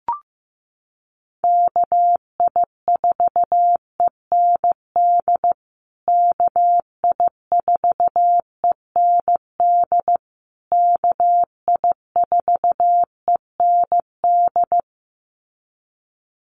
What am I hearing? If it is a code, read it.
Morse code, heard as a keyed tone, about 15 wpm, 700 Hz. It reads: KI4END KI4END KI4END
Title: USA Call Signs Repeated 3x 15wpm